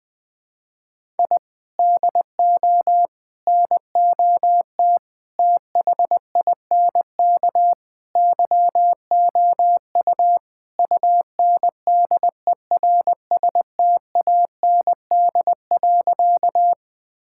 Morse code, heard as a keyed tone, about 20 wpm, 700 Hz. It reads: I DO NOT THINK YOU UNDERSTAND.
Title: Sentences with 5-Word Phrases - 20wpm